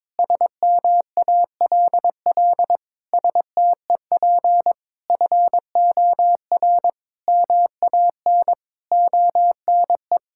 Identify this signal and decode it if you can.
Morse code, heard as a keyed tone, about 22 wpm, 700 Hz. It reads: SMALL STEP FOR MAN ONE
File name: Sets of 9 Words - 22wpm